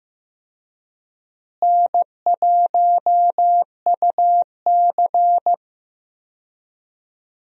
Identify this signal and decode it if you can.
Morse code, heard as a keyed tone, about 15 wpm, 700 Hz. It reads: N1UC
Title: USA Call Signs - 4 Characters - Repeated 2x - 15wpm